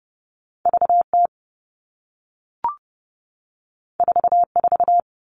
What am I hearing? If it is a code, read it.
Morse code, heard as a keyed tone, about 30 wpm, 700 Hz. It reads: VT 44
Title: All Cut Numbers - 2-Digit 30wpm